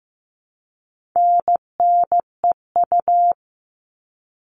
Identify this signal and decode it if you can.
Morse code, heard as a keyed tone, about 15 wpm, 700 Hz. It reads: NNEU
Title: All Cut Numbers - 4-Digit 15wpm